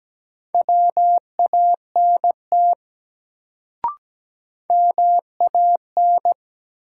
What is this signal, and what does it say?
Morse code, heard as a keyed tone, about 17 wpm, 700 Hz. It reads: WANT MAN
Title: Sets of 2 Words - New Word Drill - 17wpm